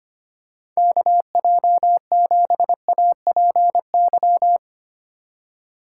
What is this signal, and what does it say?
Morse code, heard as a keyed tone, about 25 wpm, 700 Hz. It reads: KJ7APY
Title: USA Call Signs - 6 Characters - 3x Character Spacing - 25wpm